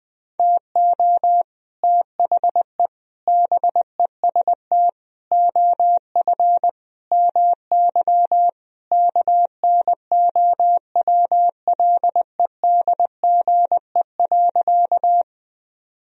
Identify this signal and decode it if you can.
Morse code, heard as a keyed tone, about 21 wpm, 700 Hz. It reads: TO THE BEST OF MY KNOWLEDGE.
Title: Sentences from Top 1000 Words - Review Part 3 - 20wpm